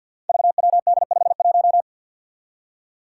Morse code, heard as a keyed tone, about 53 wpm, 700 Hz. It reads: VWD51
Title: Five Challenging Letter-Number Combinations 50wpm